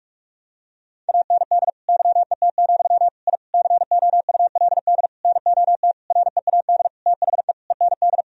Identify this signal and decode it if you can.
Morse code, heard as a keyed tone, about 46 wpm, 700 Hz. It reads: AND YET, I COULD NOT READ THE END
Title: Sentences from Top 1000 Words - Review Part 1 - 45wpm